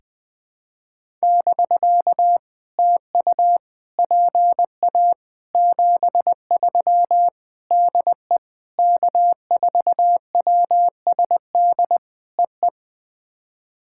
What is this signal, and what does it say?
Morse code, heard as a keyed tone, about 20 wpm, 700 Hz. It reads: <BK> TU PA 73 DE K4WSD EE
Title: POTA QSO Practice - 20wpm